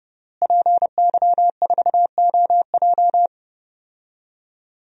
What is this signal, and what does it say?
Morse code, heard as a keyed tone, about 30 wpm, 700 Hz. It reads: PY4OJ
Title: DXCC-100 Call Signs - 5 Characters - Repeated 3x - 30wpm